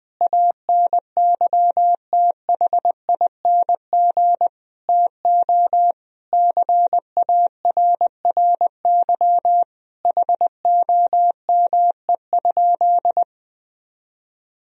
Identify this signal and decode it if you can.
Morse code, heard as a keyed tone, about 20 wpm, 700 Hz. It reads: ANYTHING TO CARRY HOME?
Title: Sentences from Top 1000 Words - Review Part 1 - 20wpm